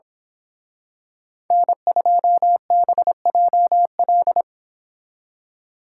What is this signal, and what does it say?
Morse code, heard as a keyed tone, about 25 wpm, 690 Hz. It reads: N2BJL
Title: Speed Racing - USA Call Signs 22wpm